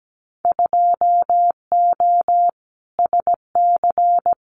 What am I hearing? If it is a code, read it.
Morse code, heard as a keyed tone, about 17 wpm, 700 Hz. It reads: 2O SC
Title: Winter Field Day Exchanges 17wpm